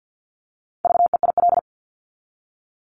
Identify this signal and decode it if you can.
Morse code, heard as a keyed tone, about 50 wpm, 700 Hz. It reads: VEIL
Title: Multi-letter Phonemes Repeated 3x 50wpm